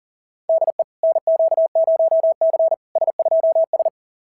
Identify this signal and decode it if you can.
Morse code, heard as a keyed tone, about 41 wpm, 690 Hz. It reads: DE NQ0C S2S
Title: SOTA QSO Practice 40wpm